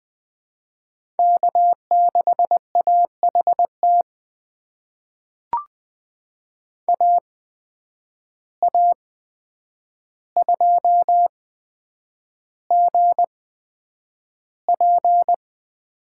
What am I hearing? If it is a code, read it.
Morse code, heard as a keyed tone, about 20 wpm, spaced as for 9 wpm, 700 Hz. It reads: K6AHT AA2GP
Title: USA Call Signs - 5 Characters - 8x Character Spacing - 20wpm